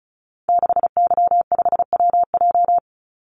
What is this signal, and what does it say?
Morse code, heard as a keyed tone, about 34 wpm, 700 Hz. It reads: 6Y5WJ
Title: Call Signs - 35wpm